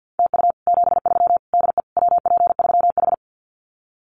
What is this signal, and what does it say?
Morse code, heard as a keyed tone, about 48 wpm, 700 Hz. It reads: TU 73 DE WP3H
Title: Common QSO Phrases Encore 50wpm